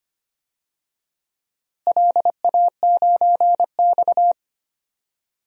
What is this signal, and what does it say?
Morse code, heard as a keyed tone, about 25 wpm, 700 Hz. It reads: LA9X
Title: DXCC-100 Call Signs - 4 Characters - Repeated 2x - 25wpm